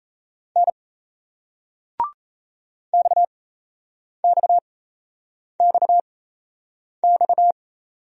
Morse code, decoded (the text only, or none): N XXXX